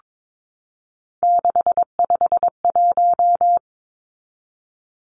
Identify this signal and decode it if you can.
Morse code, heard as a keyed tone, about 22 wpm, 700 Hz. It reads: 651
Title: Numbers - 3-Digit 22wpm